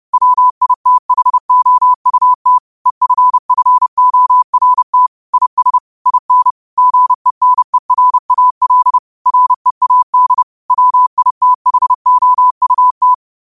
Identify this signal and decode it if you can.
Morse code, heard as a keyed tone, about 30 wpm, 1000 Hz. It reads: WITHOUT EFFORT IS IN GENERAL READ WITHOUT
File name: qotdfiles / 30wpm